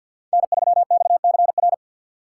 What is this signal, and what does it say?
Morse code, heard as a keyed tone, about 50 wpm, 700 Hz. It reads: N3XCR
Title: USA Call Signs - 5 Characters - 2x Character Spacing - 50wpm